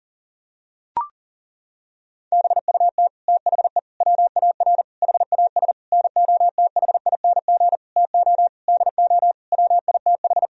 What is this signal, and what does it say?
Morse code, heard as a keyed tone, about 37 wpm, 700 Hz. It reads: BUT THE WAR HAS NOTHING TO DO WITH
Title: Sentences from Top 1000 Words - Review Part 2 - 40wpm